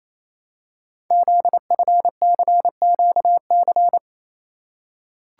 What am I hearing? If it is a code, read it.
Morse code, heard as a keyed tone, about 29 wpm, 700 Hz. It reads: ZFCQC
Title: Code Groups - 5 Letters - 28wpm